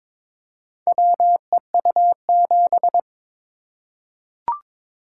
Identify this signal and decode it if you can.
Morse code, heard as a keyed tone, about 22 wpm, 700 Hz. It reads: WEU7
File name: practice / 22wpm / Four Letter-Number Combinations - 22wpm